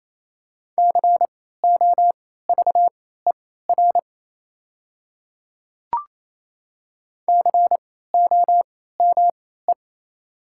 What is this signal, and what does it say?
Morse code, heard as a keyed tone, about 28 wpm, spaced as for 16 wpm, 700 Hz. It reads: COVER COME
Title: 3x Character Spacing - Top 300 Words - 28wpm